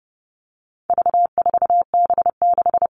Morse code, heard as a keyed tone, about 30 wpm, 700 Hz. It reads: V4B6
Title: All Cut Numbers - 4-Digit 30wpm